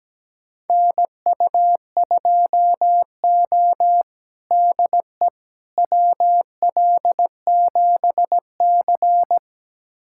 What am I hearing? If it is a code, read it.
Morse code, heard as a keyed tone, about 17 wpm, 700 Hz. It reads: NU2O DE WL7C
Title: USA Callsign DE Callsign - 4 Characters - 17wpm